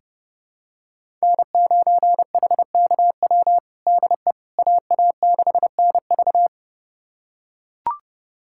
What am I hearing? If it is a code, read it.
Morse code, heard as a keyed tone, about 30 wpm, 700 Hz. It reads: N9HKW DE AA6NV
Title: USA Callsign DE Callsign - 5 Characters - 30wpm